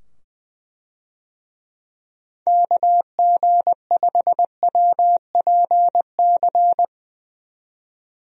Morse code, heard as a keyed tone, about 21 wpm, 700 Hz. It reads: KG5WPC
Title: USA Call Signs - 6 Characters - 20wpm